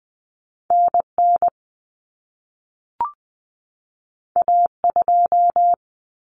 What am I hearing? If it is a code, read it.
Morse code, heard as a keyed tone, about 20 wpm, 700 Hz. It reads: NN A2
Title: Common Cut Numbers - 2-Digit 20wpm